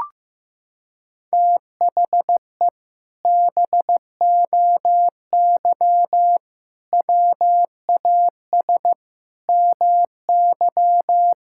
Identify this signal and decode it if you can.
Morse code, heard as a keyed tone, about 14 wpm, 700 Hz. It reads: THE BOY WAS MY
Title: Sentences from Top 1600 Words 15wpm